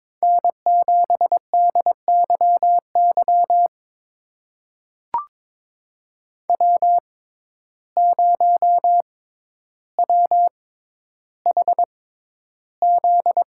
Morse code, heard as a keyed tone, about 22 wpm, spaced as for 11 wpm, 700 Hz. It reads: N7DYY W0WHZ
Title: USA Call Signs - 5 Characters - 6x Character Spacing - 22wpm